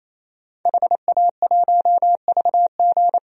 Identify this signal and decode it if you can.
Morse code, heard as a keyed tone, about 28 wpm, 700 Hz. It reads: HA1VG
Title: DXCC-100 Call Signs - 5 Characters - 28wpm